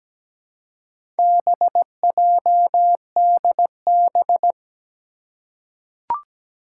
Morse code, heard as a keyed tone, about 17 wpm, 700 Hz. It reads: BJDB
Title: Four Challenging Letter-Number Combinations 17wpm